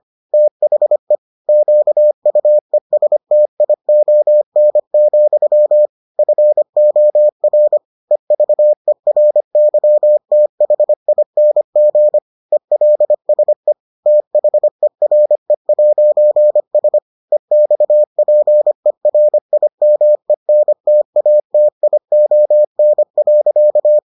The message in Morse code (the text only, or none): THE QUESTION, FOR EVERYTHING ELSE THERE'S EXPERIMENTATION.